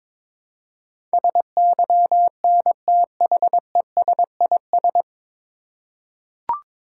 Morse code, heard as a keyed tone, about 22 wpm, 700 Hz. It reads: SYNTHESIS
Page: Morse Code Ninja practice 22wpm